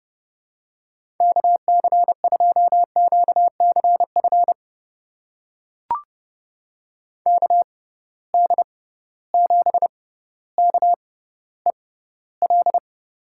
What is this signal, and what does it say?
Morse code, heard as a keyed tone, about 30 wpm, spaced as for 15 wpm, 700 Hz. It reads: KC2QCF KD7KEL
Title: USA Call Signs - 6 Characters - 6x Character Spacing - 30wpm